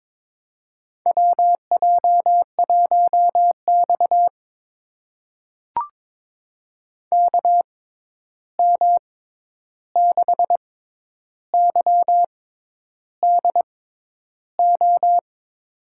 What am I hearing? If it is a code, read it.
Morse code, heard as a keyed tone, about 22 wpm, spaced as for 10 wpm, 700 Hz. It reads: WJ1X KM6YDO
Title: USA Call Signs - 6x Character Spacing - 22wpm